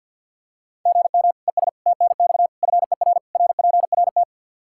Morse code, heard as a keyed tone, about 49 wpm, 700 Hz. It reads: GM ES TNX FER RPRT
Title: Practice QSO - Exchange 1 - Beginning - 50wpm